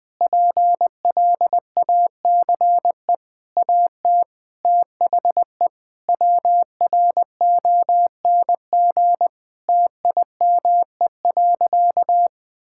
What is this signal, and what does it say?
Morse code, heard as a keyed tone, about 20 wpm, 700 Hz. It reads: PLACE AT THE WRONG TIME.
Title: Sentences with 7-Word Phrases - Spoken Beginning - 20wpm